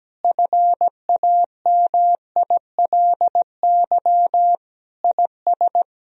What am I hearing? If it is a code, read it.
Morse code, heard as a keyed tone, about 17 wpm, 700 Hz. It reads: FAMILY IS
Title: Sentences from Top 1000 Words - Review Part 2 - 17wpm